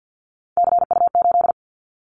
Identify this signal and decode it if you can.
Morse code, heard as a keyed tone, about 50 wpm, 700 Hz. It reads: CU8
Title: Callsign Prefixes for Top 100 DXCC - 50wpm